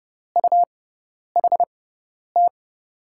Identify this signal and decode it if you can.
Morse code, heard as a keyed tone, about 30 wpm, spaced as for 10 wpm, 700 Hz. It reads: UHT
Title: USA Call Signs - 6x Character Spacing - 30wpm